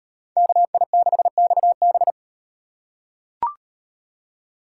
Morse code, heard as a keyed tone, about 37 wpm, 700 Hz. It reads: KI6XB